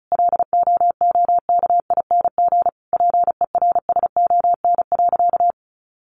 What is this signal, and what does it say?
Morse code, heard as a keyed tone, about 34 wpm, 700 Hz. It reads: LOOKING PERSON.
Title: Easy Phrases 35wpm